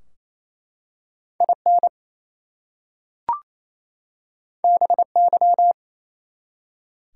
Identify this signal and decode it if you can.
Morse code, heard as a keyed tone, about 29 wpm, 700 Hz. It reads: IN BY
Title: Two Letter Words - 28wpm